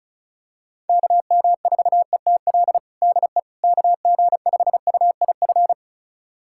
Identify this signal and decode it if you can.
Morse code, heard as a keyed tone, about 36 wpm, 700 Hz. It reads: KM4ETL DE KG5UIF
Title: USA Callsign DE Callsign - 6 Characters - 35wpm